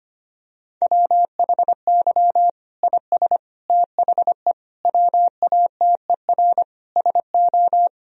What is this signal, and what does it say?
Morse code, heard as a keyed tone, about 25 wpm, 700 Hz. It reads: WHY IS THE WATER SO
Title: Sentences from Top 1000 Words - Review Part 2 - 25wpm